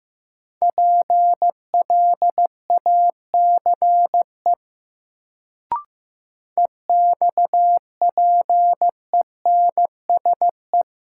Morse code, heard as a keyed tone, about 15 wpm, 700 Hz. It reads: PLACE EXPENSE
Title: Sets of 4 Words - New Word Drill - 15wpm